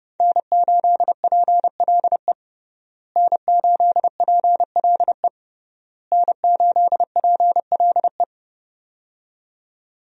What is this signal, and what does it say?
Morse code, heard as a keyed tone, about 30 wpm, 700 Hz. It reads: N8PLE N8PLE N8PLE
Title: USA Call Signs - 5 Characters - Repeated 3x - 30wpm